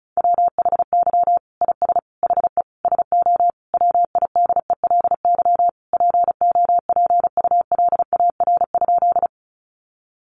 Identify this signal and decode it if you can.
Morse code, heard as a keyed tone, about 34 wpm, 700 Hz. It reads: WHY IS HE SO WIDELY POPULAR?
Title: Sentences from Top 1200 Words 35wpm